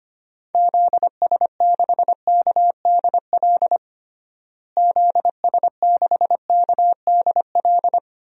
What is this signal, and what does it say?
Morse code, heard as a keyed tone, about 25 wpm, 700 Hz. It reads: ZS6KDL ZS6KDL
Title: DXCC-100 Call Signs - 6 Characters - Repeated 2x - 25wpm